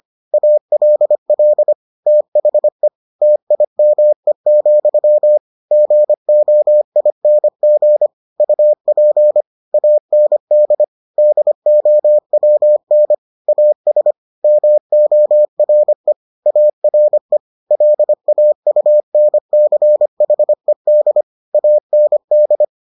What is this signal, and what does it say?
Morse code, heard as a keyed tone, about 25 wpm, 600 Hz. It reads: ALL THE TIME, GOING UP AND DOWN AS MORE ARE LAUNCHED AND